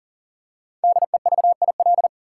Morse code, heard as a keyed tone, about 40 wpm, 700 Hz. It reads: DEVIL
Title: Five Letter Words - 40wpm